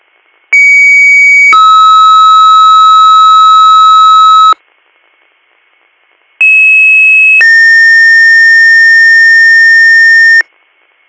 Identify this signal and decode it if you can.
Two-tone page, heard as a one-second tone, then a three-second tone, about 2250 then 1290 Hz, about 2600 then 1800 Hz. It is heard from 0.5 to 10.4 s.